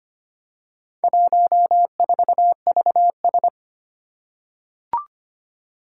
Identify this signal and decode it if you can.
Morse code, heard as a keyed tone, about 25 wpm, 700 Hz. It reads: 14VS
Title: Four Challenging Letter-Number Combinations 25wpm